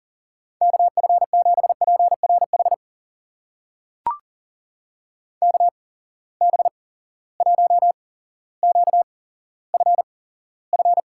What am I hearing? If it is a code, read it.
Morse code, heard as a keyed tone, about 40 wpm, spaced as for 17 wpm, 700 Hz. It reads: KF7PRH KB1QFF
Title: USA Call Signs - 6 Characters - 8x Character Spacing - 40wpm